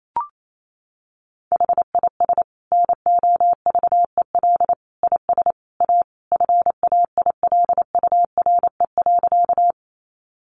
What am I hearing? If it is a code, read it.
Morse code, heard as a keyed tone, about 28 wpm, 700 Hz. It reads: HIS NOVEL IS A FAILURE.
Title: Sentences from Top 1300 Words 28wpm